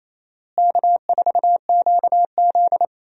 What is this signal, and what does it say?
Morse code, heard as a keyed tone, about 28 wpm, 700 Hz. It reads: K4QZ
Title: USA Call Signs - 4 Characters - Repeated 3x - 28wpm